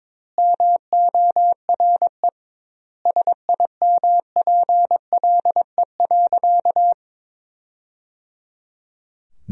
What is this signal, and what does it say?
Morse code, heard as a keyed tone, about 22 wpm, 700 Hz. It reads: MORE SIMPLE.
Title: 2x Word-Spacing - Sentences from Top 500 Words - 22wpm